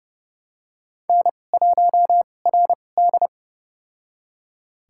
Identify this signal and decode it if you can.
Morse code, heard as a keyed tone, about 30 wpm, spaced as for 22 wpm, 700 Hz. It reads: N1RD
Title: USA Call Signs - 4 Characters - 2x Character Spacing - 30wpm